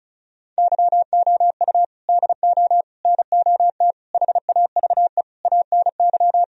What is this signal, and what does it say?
Morse code, heard as a keyed tone, about 34 wpm, 700 Hz. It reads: YOU DO NOT HAVE ANY
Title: Sentences from Top 1000 Words - Review Part 4 - 35wpm